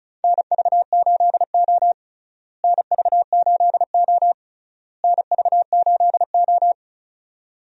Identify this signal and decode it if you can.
Morse code, heard as a keyed tone, about 36 wpm, 700 Hz. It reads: NV8O NV8O NV8O
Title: USA Call Signs - 4 Characters - Repeated 3x - 35wpm